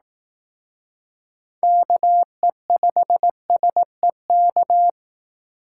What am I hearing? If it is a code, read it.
Morse code, heard as a keyed tone, about 18 wpm, 700 Hz. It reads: KE5SEK
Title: Speed Racing - USA 2x3 Call Signs 15wpm